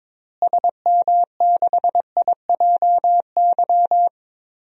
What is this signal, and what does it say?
Morse code, heard as a keyed tone, about 22 wpm, 700 Hz. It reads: SM6IJY
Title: DXCC-100 Call Signs - 6 Characters - Repeated 2x - 22wpm